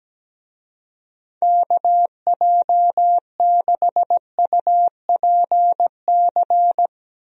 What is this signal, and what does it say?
Morse code, heard as a keyed tone, about 17 wpm, 700 Hz. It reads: KJ6UPC
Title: USA Call Signs - 6 Characters - Repeated 2x - 17wpm